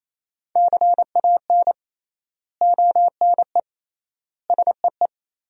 Morse code, heard as a keyed tone, about 28 wpm, 700 Hz. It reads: CAN ONE SEE
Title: 3x Word-Spacing - Sentences from Top 100 Words - 28wpm